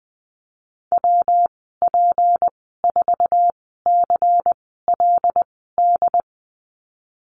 Morse code, heard as a keyed tone, about 20 wpm, spaced as for 14 wpm, 700 Hz. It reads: WP4CLD
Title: 161 Lesson 26g - Letter g - Call Signs 20wpm